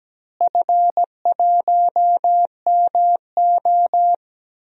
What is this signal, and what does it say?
Morse code, heard as a keyed tone, about 17 wpm, 700 Hz. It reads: F1MO
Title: DXCC-100 Call Signs - 4 Characters - Repeated 3x - 17wpm